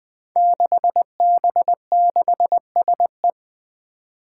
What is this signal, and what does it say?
Morse code, heard as a keyed tone, about 20 wpm, 700 Hz. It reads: 6B6SE
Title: Five Challenging Letter-Number Combinations 20wpm